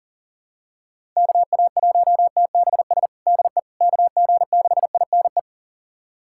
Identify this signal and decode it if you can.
Morse code, heard as a keyed tone, about 40 wpm, 700 Hz. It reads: KA1TBS DE KG6INE
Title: USA Callsign DE Callsign 40wpm